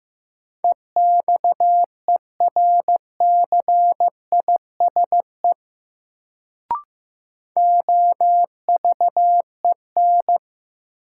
Morse code, heard as a keyed tone, about 15 wpm, 700 Hz. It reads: EXERCISE OVEN
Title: Sentences from Top 1000 Words - Review Part 4 - New Word Drill - 15wpm